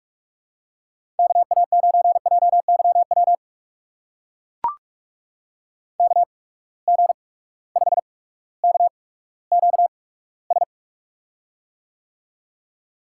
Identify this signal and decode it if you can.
Morse code, heard as a keyed tone, about 44 wpm, spaced as for 19 wpm, 700 Hz. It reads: KA9JYW KG5KQS
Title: USA Call Signs - 6 Characters - 8x Character Spacing - 45wpm